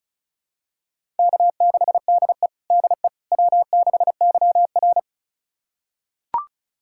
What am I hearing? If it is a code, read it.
Morse code, heard as a keyed tone, about 36 wpm, 700 Hz. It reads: K6DE DE W6YR
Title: USA Callsign DE Callsign - 4 Characters - 35wpm